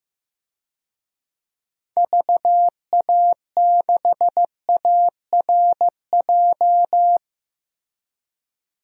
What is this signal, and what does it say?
Morse code, heard as a keyed tone, about 16 wpm, 700 Hz. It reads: VA6ARJ
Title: DXCC-100 Call Signs - 6 Characters - Repeated 2x - 15wpm